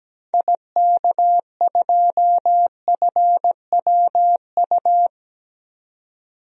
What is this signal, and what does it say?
Morse code, heard as a keyed tone, about 17 wpm, 700 Hz. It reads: IK2FWU
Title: DXCC-100 Call Signs - 6 Characters - 17wpm